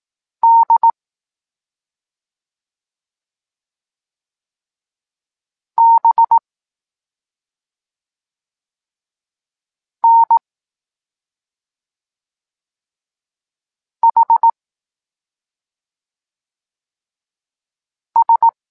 Morse code, heard as a keyed tone, about 18 wpm, 920 Hz. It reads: D B N H S
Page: Tekst 14 – doskonalenie odbioru poznanych liter